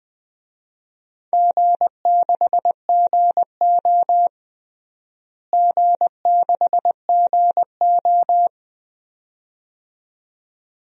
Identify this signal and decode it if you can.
Morse code, heard as a keyed tone, about 21 wpm, 700 Hz. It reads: G6GO G6GO
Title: DXCC-100 Call Signs - 4 Characters - Repeated 2x - 20wpm